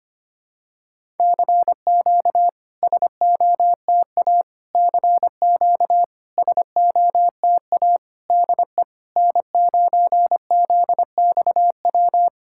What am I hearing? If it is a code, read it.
Morse code, heard as a keyed tone, about 25 wpm, 700 Hz. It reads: CQ SOTA CQ SOTA DE N9ZXW
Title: SOTA QSO Practice 25wpm